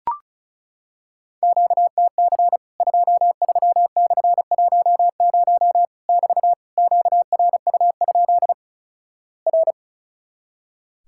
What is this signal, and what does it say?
Morse code, heard as a keyed tone, about 35 wpm, 700 Hz. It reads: QTC 23/10 <BT> QRU? R
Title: Worked All Europe DX Contest - QTC Exchange - 35wpm